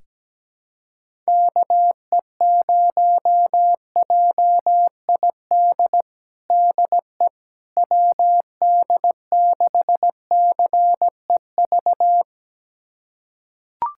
Morse code, heard as a keyed tone, about 17 wpm, 700 Hz. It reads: KE0JID DE WD6CEV
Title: USA Callsign DE Callsign - 6 Characters - 17wpm